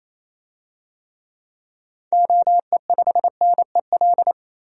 Morse code, heard as a keyed tone, about 28 wpm, 700 Hz. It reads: OE5NEL
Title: DXCC-100 Call Signs - 6 Characters - 28wpm